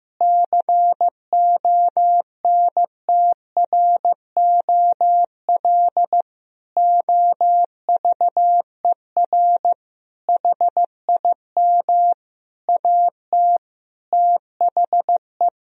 Morse code, heard as a keyed tone, about 15 wpm, 700 Hz. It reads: CONTROL OVER HIM AT THE
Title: Sentences from Top 1000 Words - Review Part 3 - 15wpm